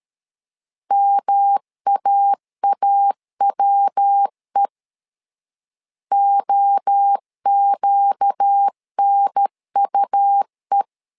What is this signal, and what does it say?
Morse code, heard as a keyed tone, about 13 wpm, 800 Hz. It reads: MAAWE OQNUE
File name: LEZIONI / 4 Q 12wpm 1833